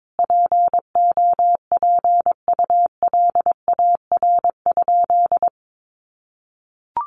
Morse code, heard as a keyed tone, about 22 wpm, 700 Hz. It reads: POPULAR?
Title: Sentences from Top 1200 Words 22wpm